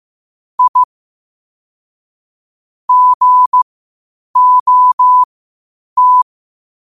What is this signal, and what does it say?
Morse code, heard as a keyed tone, about 15 wpm, spaced as for 8 wpm, 1000 Hz. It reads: I GOT
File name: qotdfiles / 8wpm